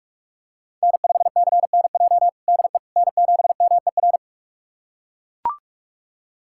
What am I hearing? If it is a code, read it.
Morse code, heard as a keyed tone, about 42 wpm, 700 Hz. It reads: N5CNJ DE N7MER
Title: USA Callsign DE Callsign - 5 Characters - 45wpm